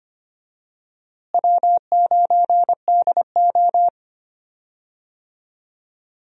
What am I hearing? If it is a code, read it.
Morse code, heard as a keyed tone, about 25 wpm, 700 Hz. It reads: W9DO
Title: USA Call Signs - 4 Characters - Repeated 3x - 25wpm